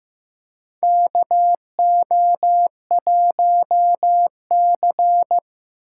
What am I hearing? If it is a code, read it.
Morse code, heard as a keyed tone, about 15 wpm, 700 Hz. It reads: KO1C
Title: USA Call Signs - 6x Character Spacing - 15wpm